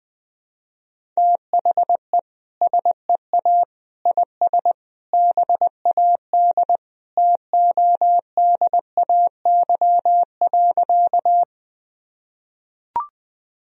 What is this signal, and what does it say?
Morse code, heard as a keyed tone, about 20 wpm, 700 Hz. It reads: THE SEA IS BAD TODAY.
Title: Sentences from Top 1000 Words - Review Part 1 - 20wpm